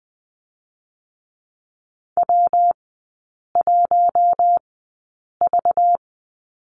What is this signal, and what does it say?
Morse code, heard as a keyed tone, about 20 wpm, spaced as for 8 wpm, 700 Hz. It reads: W1V
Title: 198 Lesson 32e - Letter k - Three Characters 20wpm